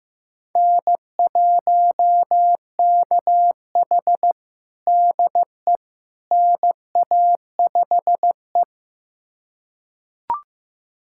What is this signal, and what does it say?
Morse code, heard as a keyed tone, about 15 wpm, 700 Hz. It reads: N1KH DE NA5E
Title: USA Callsign DE Callsign - 4 Characters - 15wpm